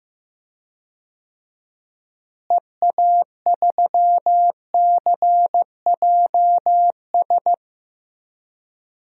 Morse code, heard as a keyed tone, about 15 wpm, 700 Hz. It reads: EA3CJS
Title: DXCC-100 Call Signs - 6 Characters - Repeated 2x - 15wpm